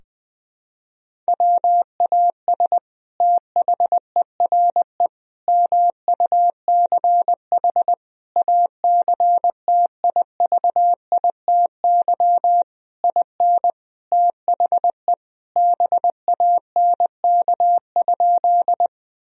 Morse code, heard as a keyed tone, about 20 wpm, 700 Hz. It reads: WAS THERE MUCH ACTIVITY IN THE BANK?
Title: Sentences from Top 1000 Words - Review Part 4 - 20wpm